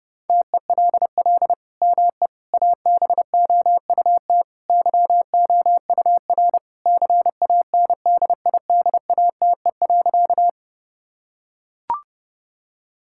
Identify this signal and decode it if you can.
Morse code, heard as a keyed tone, about 30 wpm, 700 Hz. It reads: TELL ME ABOUT YOUR CANDIDATE.
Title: Sentences from Top 1600 Words 30wpm